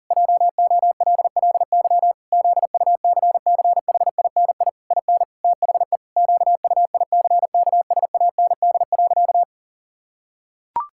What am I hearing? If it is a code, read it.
Morse code, heard as a keyed tone, about 37 wpm, 700 Hz. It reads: JOLLY ZUCCHINI IN THE QUICKSAND.
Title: Pangram Challenge v2 - 40wpm